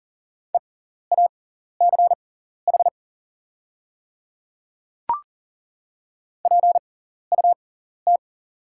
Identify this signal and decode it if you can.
Morse code, heard as a keyed tone, about 40 wpm, spaced as for 14 wpm, 690 Hz. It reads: EACH PUT
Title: 6x Character Spacing - Top 200 Words - 40wpm